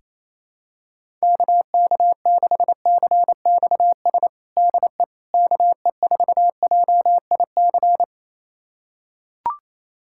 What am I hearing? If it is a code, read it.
Morse code, heard as a keyed tone, about 28 wpm, 700 Hz. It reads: KK6CXS DE KE4JIC
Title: USA Callsign DE Callsign - 6 Characters - 28wpm